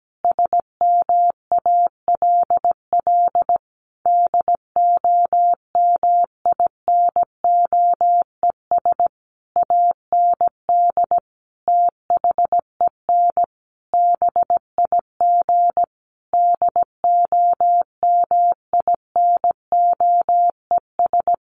Morse code, heard as a keyed tone, about 17 wpm, 700 Hz. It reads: SMALL DOMINOES AND THEN BIG DOMINOES
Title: A Sound Of Thunder 17wpm